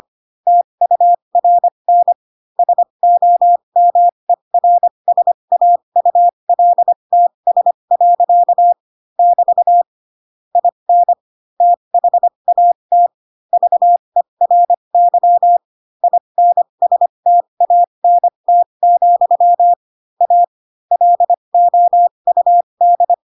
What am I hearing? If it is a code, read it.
Morse code, heard as a keyed tone, about 25 wpm, spaced as for 22 wpm, 700 Hz. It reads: TURN SOMERSAULTS. <BT> IN THAT VERY INSTANT, A LOUD
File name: Pinocchio / Pinocchio ch2 18 25 ik1hge